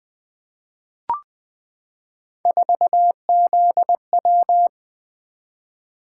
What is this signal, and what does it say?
Morse code, heard as a keyed tone, about 20 wpm, 700 Hz. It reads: 4ZW
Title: Three Letter-Number Combinations - 20wpm